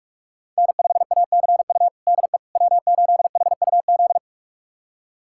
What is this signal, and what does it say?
Morse code, heard as a keyed tone, about 46 wpm, 700 Hz. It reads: N5ACU DE W8HUZ
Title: USA Callsign DE Callsign - 5 Characters - 45wpm